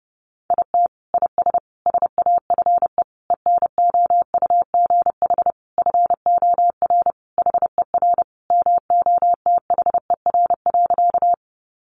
Morse code, heard as a keyed tone, about 30 wpm, 700 Hz. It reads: IT IS SAFE ENOUGH FOR HER MOTHER.
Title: Sentences from Top 900 Words 30wpm